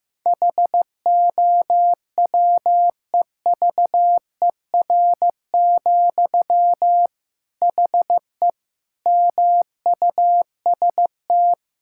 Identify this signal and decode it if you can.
Morse code, heard as a keyed tone, about 15 wpm, 700 Hz. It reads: HOWEVER, HE MUST
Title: Sentences from Top 1600 Words 15wpm